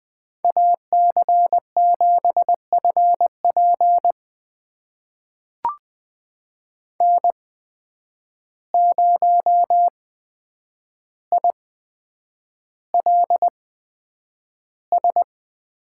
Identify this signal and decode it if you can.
Morse code, heard as a keyed tone, about 20 wpm, spaced as for 9 wpm, 700 Hz. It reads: AC7FP N0ILS
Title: USA Call Signs - 5 Characters - 8x Character Spacing - 20wpm